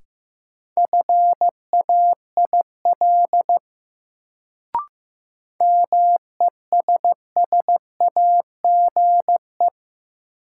Morse code, heard as a keyed tone, about 15 wpm, 700 Hz. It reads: FAIL MESSAGE
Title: Sentences from Top 1100 Words - New Word Drill - 15wpm